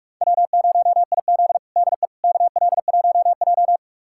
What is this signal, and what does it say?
Morse code, heard as a keyed tone, about 45 wpm, 700 Hz. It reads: W0IZ DE KL1J
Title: USA Callsign DE Callsign - 4 Characters - 45wpm